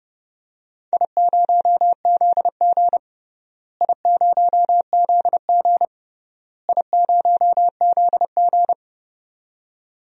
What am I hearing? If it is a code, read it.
Morse code, heard as a keyed tone, about 32 wpm, 700 Hz. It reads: I0ZG I0ZG I0ZG
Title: DXCC-100 Call Signs - 4 Characters - Repeated 3x - 30wpm